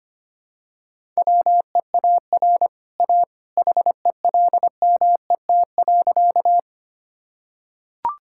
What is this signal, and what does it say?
Morse code, heard as a keyed tone, about 25 wpm, 690 Hz. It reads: WEAR A HELMET.
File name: Easy Sentences with Hints - 25wpm